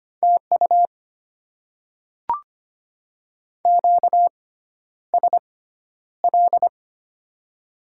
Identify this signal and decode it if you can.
Morse code, heard as a keyed tone, about 25 wpm, spaced as for 11 wpm, 700 Hz. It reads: TU QSL
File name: Top 90 QSO Elements - 6x Character Spacing - 25wpm